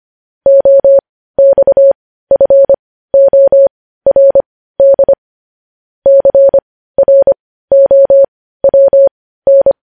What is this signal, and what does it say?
Morse code, heard as a keyed tone, about 25 wpm, spaced as for 15 wpm, 550 Hz. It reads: OXFORD CROWN
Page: news-headlines-in-morse-code-at-15-wpm